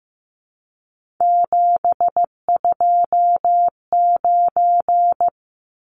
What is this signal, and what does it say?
Morse code, heard as a keyed tone, about 15 wpm, 700 Hz. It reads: 729
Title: Numbers - 3-Digit 15wpm